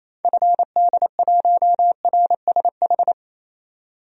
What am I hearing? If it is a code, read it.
Morse code, heard as a keyed tone, about 28 wpm, 700 Hz. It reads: FD1RSH
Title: DXCC-100 Call Signs - 6 Characters - 28wpm